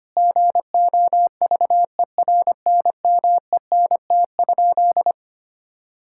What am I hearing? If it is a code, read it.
Morse code, heard as a keyed tone, about 25 wpm, 700 Hz. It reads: GOVERNMENT?
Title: Sentences from Top 1000 Words - Review Part 5 - 25wpm